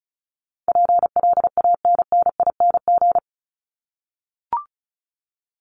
Morse code, heard as a keyed tone, about 34 wpm, 700 Hz. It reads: PLANNING
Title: ING Suffix - Top 100 Words with Base Words 35wpm